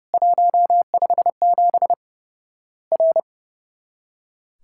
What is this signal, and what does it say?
Morse code, heard as a keyed tone, about 30 wpm, 700 Hz. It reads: 157 R
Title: Worked All Europe DX Contest - QTC Exchange - 30wpm